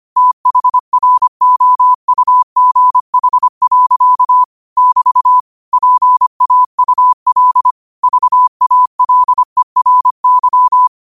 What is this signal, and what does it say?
Morse code, heard as a keyed tone, about 25 wpm, 1000 Hz. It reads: THROUGH. <BT> PAUL VALERY
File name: qotdfiles / 25wpm